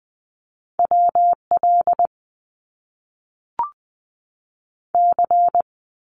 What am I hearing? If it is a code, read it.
Morse code, heard as a keyed tone, about 20 wpm, 700 Hz. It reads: WL C
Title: Abbreviations 20wpm